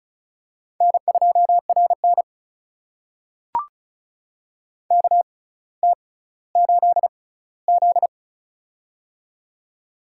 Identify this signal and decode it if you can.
Morse code, heard as a keyed tone, about 36 wpm, spaced as for 18 wpm, 700 Hz. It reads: N2RN KT8Z
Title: USA Call Signs - 4 Characters - 6x Character Spacing - 35wpm